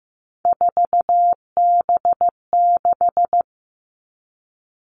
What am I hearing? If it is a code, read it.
Morse code, heard as a keyed tone, about 15 wpm, 700 Hz. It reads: 4B6
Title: All Cut Numbers - 3-Digit 15wpm